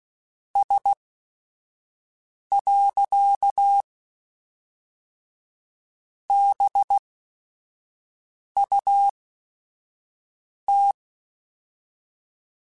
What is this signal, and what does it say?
Morse code, heard as a keyed tone, about 16 wpm, spaced as for 5 wpm, 800 Hz. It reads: S. BUT
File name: Technical Subjects / Technical Subjects-mp3s / AND Gate @5